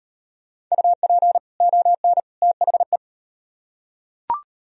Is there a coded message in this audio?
Yes: Morse code from 0.7 to 3.0 s, heard as a keyed tone, about 38 wpm, 700 Hz. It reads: UP ON THE